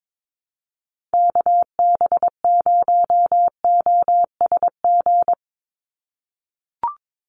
Morse code, heard as a keyed tone, about 22 wpm, 700 Hz. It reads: KB0OSG
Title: USA Call Signs 22wpm